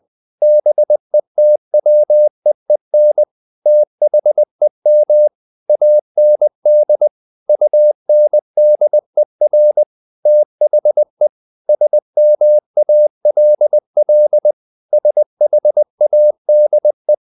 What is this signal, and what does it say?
Morse code, heard as a keyed tone, about 20 wpm, 600 Hz. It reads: BETWEEN THEM AND UNDER THE SMALL SHADE
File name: the old man and the sea / the old man and the sea part 1 20wpm